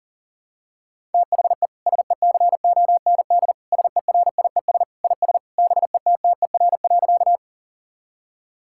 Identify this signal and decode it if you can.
Morse code, heard as a keyed tone, about 40 wpm, 700 Hz. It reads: THE SECOND SERIES IS BETTER.
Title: Sentences from Top 1000 Words - Review Part 5 - 40wpm